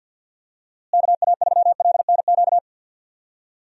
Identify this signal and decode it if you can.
Morse code, heard as a keyed tone, about 53 wpm, 700 Hz. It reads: KA3LNQ
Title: USA Call Signs - 6 Characters - Repeated 2x - 50wpm